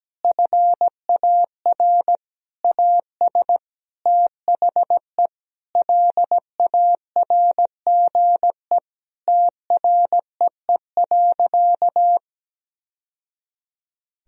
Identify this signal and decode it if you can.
Morse code, heard as a keyed tone, about 17 wpm, 700 Hz. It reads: FAR AS THE LARGE TREE.
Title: Sentences from Top 1000 Words - Review Part 4 - 17wpm